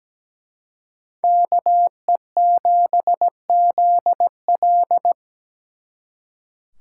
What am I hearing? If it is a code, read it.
Morse code, heard as a keyed tone, about 20 wpm, 700 Hz. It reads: KE7ZL
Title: Speed Racing - USA 2x2 Call Signs 17wpm